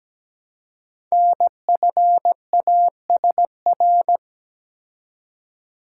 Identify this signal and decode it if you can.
Morse code, heard as a keyed tone, about 17 wpm, 700 Hz. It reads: NFASR
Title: Code Groups - 5 Letters - 17wpm